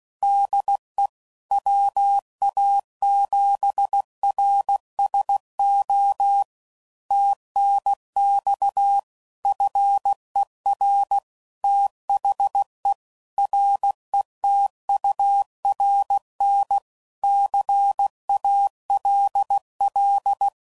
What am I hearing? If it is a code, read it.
Morse code, heard as a keyed tone, about 16 wpm, 800 Hz. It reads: DE WA7RSO TNX FER THE RETURN CALL
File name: CW Code Groups @15WPM / QSOs / kc7sac